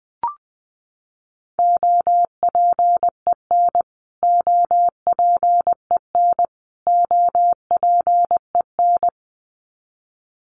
Morse code, heard as a keyed tone, about 20 wpm, 700 Hz. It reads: OPEN OPEN OPEN
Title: Top 500 Words Repeated 20wpm